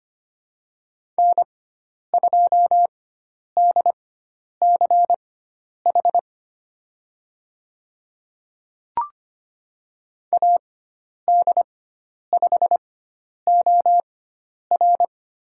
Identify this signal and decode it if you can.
Morse code, heard as a keyed tone, about 25 wpm, spaced as for 10 wpm, 700 Hz. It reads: N2DCH AD5OR
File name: Lesson 16g - Letter c - Call Signs 25wpm